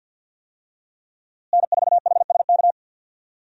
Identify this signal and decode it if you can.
Morse code, heard as a keyed tone, about 52 wpm, 700 Hz. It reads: N4HSK
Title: USA Call Signs - 5 Characters - Repeated 3x - 50wpm